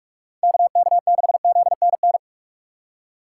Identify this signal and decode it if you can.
Morse code, heard as a keyed tone, about 46 wpm, 700 Hz. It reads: KK6ZNN
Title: USA Call Signs - 6 Characters - 4x Character Spacing - 45wpm